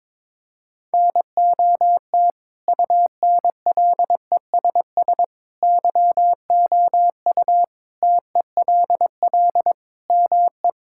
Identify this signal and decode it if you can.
Morse code, heard as a keyed tone, about 22 wpm, 700 Hz. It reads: NOT UNLESS YOU TELL ME
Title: Sentences from Top 1000 Words - Review Part 6 - 22wpm